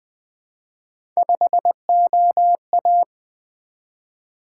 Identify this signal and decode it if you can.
Morse code, heard as a keyed tone, about 20 wpm, 700 Hz. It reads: 5OA
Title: Three Letter-Number Combinations - 20wpm